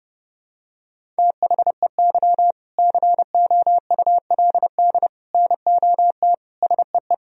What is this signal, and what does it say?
Morse code, heard as a keyed tone, about 32 wpm, 700 Hz. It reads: THEY COULD NOT SEE
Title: Sentences from Top 1000 Words - Review Part 1 - 30wpm